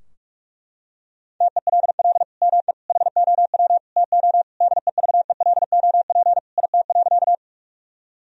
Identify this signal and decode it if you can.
Morse code, heard as a keyed tone, about 41 wpm, 700 Hz. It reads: TELL ME HOW TO DEVELOP IT.
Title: Sentences from Top 1000 Words - Review Part 2 - 45wpm